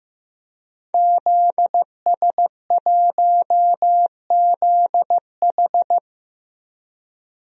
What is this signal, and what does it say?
Morse code, heard as a keyed tone, about 15 wpm, 700 Hz. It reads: ZS1ZH
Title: DXCC-100 Call Signs - 5 Characters - Repeated 2x - 15wpm